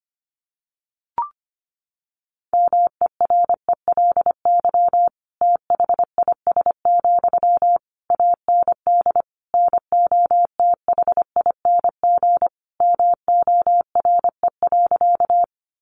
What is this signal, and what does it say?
Morse code, heard as a keyed tone, about 25 wpm, 700 Hz. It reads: MERELY THIS, AND NOTHING MORE.
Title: Sentences from Top 1400 Words 25wpm